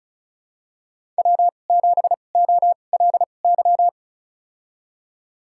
Morse code, heard as a keyed tone, about 36 wpm, spaced as for 25 wpm, 700 Hz. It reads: W7OLY
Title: USA Call Signs - 5 Characters - 2x Character Spacing - 35wpm